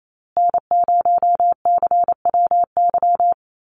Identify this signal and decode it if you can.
Morse code, heard as a keyed tone, about 28 wpm, 700 Hz. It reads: N0CWY
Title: USA Call Signs Repeated 3x 28wpm